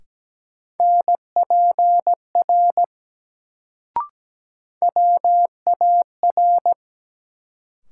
Morse code, heard as a keyed tone, about 17 wpm, 700 Hz. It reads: NPR WAR
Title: Sets of 5 Words - New Word Drill - 17wpm